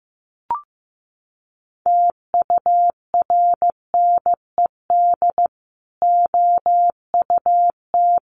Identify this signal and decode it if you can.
Morse code, heard as a keyed tone, about 15 wpm, 700 Hz. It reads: TURNED OUT
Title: Sets of 4 Words 15wpm